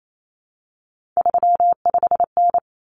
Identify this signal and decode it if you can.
Morse code, heard as a keyed tone, about 28 wpm, 700 Hz. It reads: 35N
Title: Common Cut Numbers - 3-Digit 28wpm